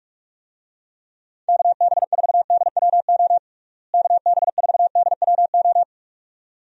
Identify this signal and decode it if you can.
Morse code, heard as a keyed tone, about 46 wpm, 700 Hz. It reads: KB4DWO KB4DWO
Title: USA Call Signs - 6 Characters - Repeated 2x - 45wpm